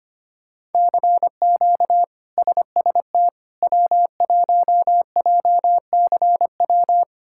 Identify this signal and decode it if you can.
Morse code, heard as a keyed tone, about 25 wpm, 700 Hz. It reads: CQ SST W1JCW
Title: K1USN SST QSO Practice - 25wpm